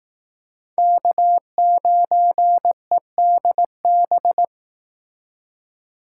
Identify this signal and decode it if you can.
Morse code, heard as a keyed tone, about 18 wpm, 700 Hz. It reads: K9EDB
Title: Speed Racing - USA 1x3 Call Signs 15wpm